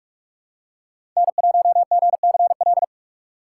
Morse code, heard as a keyed tone, about 46 wpm, 700 Hz. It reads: N1GCL
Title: USA Call Signs - 5 Characters - 6x Character Spacing - 45wpm